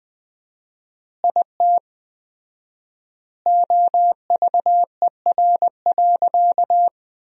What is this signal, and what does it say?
Morse code, heard as a keyed tone, about 20 wpm, 700 Hz. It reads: IT OVER.
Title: 4x Word-Spacing - Sentences from Top 500 Words - 20wpm